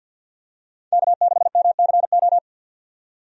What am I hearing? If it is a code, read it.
Morse code, heard as a keyed tone, about 48 wpm, 700 Hz. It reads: K6MCO
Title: USA Call Signs - 5 Characters - 50wpm